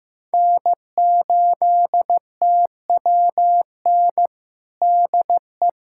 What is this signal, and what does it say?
Morse code, heard as a keyed tone, about 15 wpm, 700 Hz. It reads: N8TWN DE
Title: USA Callsign DE Callsign 15wpm